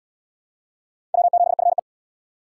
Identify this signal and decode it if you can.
Morse code, heard as a keyed tone, about 76 wpm, 700 Hz. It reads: W7FE